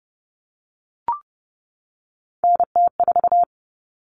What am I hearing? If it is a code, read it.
Morse code, heard as a keyed tone, about 30 wpm, 700 Hz. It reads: NT4
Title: Common Cut Numbers - 3-Digit 30wpm